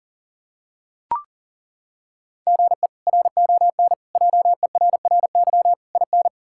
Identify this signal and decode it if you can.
Morse code, heard as a keyed tone, about 40 wpm, 700 Hz. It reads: GE RON JERRY IN
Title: K1USN SST QSO Practice - 40wpm